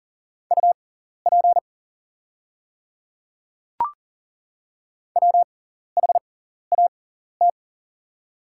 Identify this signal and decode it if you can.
Morse code, heard as a keyed tone, about 42 wpm, spaced as for 14 wpm, 690 Hz. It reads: UP WHAT
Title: 6x Character Spacing - Top 100 Words - 40wpm